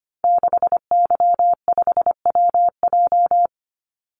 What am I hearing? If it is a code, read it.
Morse code, heard as a keyed tone, about 25 wpm, 700 Hz. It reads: 6Y5WJ
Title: Call Signs - 25wpm